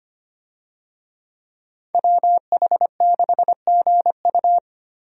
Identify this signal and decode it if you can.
Morse code, heard as a keyed tone, about 25 wpm, 690 Hz. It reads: WH6GU
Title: USA Call Signs - 5 Characters - 25wpm